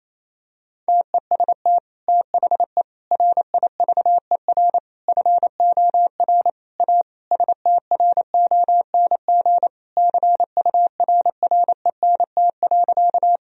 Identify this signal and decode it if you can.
Morse code, heard as a keyed tone, about 28 wpm, 700 Hz. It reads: TEST THE RIVER FOR A STRONG CURRENT.
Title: Sentences from Top 1000 Words - Review Part 4 - 28wpm